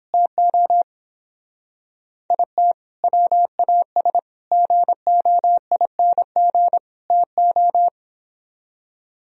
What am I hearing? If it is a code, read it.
Morse code, heard as a keyed tone, about 31 wpm, 700 Hz. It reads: TO IT WAS GOING TO